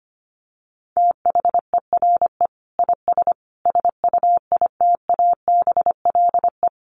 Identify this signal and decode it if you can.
Morse code, heard as a keyed tone, about 25 wpm, 700 Hz. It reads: THERE IS SUITABLE
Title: Sentences from Top 1500 Words 25wpm